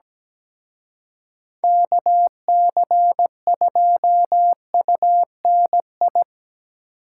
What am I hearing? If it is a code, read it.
Morse code, heard as a keyed tone, about 20 wpm, 700 Hz. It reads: KC2UNI
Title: Speed Racing - USA 2x3 Call Signs 17wpm